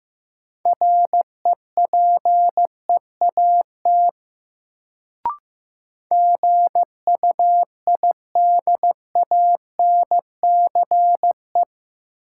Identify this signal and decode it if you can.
Morse code, heard as a keyed tone, about 15 wpm, 700 Hz. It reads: REPEAT GUIDANCE
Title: Sentences from Top 1600 Words - New Word Drill Alt - 15wpm